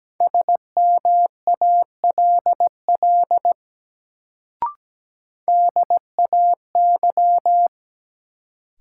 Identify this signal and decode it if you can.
Morse code, heard as a keyed tone, about 17 wpm, 700 Hz. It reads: SMALL DAY
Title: Sentences from Top 200 Words - New Word Drill - 17wpm